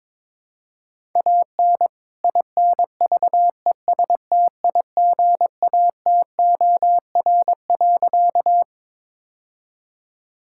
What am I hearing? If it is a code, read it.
Morse code, heard as a keyed tone, about 22 wpm, 700 Hz. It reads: AN INVESTIGATOR.
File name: All-Age Appropriate Jokes - 22wpm